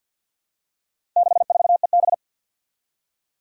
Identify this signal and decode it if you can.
Morse code, heard as a keyed tone, about 47 wpm, 700 Hz. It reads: 64EB
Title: Four Challenging Letter-Number Combinations 50wpm